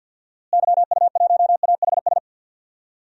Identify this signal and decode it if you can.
Morse code, heard as a keyed tone, about 52 wpm, 700 Hz. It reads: YU1AHS
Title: DXCC-100 Call Signs - 6 Characters - Repeated 3x - 50wpm